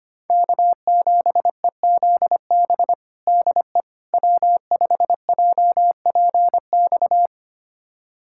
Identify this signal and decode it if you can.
Morse code, heard as a keyed tone, about 25 wpm, 700 Hz. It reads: K7EZB DE W5JPX
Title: USA Callsign DE Callsign - 5 Characters - 25wpm